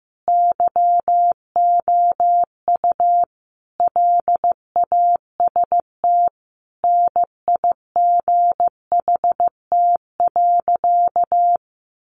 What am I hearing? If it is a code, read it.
Morse code, heard as a keyed tone, about 15 wpm, 700 Hz. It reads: YOU LAST NIGHT.
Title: Sentences from Top 1300 Words 15wpm